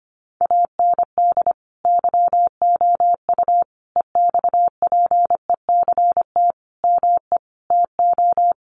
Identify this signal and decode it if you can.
Morse code, heard as a keyed tone, about 25 wpm, 700 Hz. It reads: AND YOU EXPECT ME TO
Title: Sentences from Top 1100 Words 25wpm